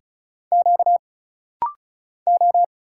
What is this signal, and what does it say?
Morse code, heard as a keyed tone, about 36 wpm, 700 Hz. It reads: Q O
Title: Single Letters - Rapid-Fire 35wpm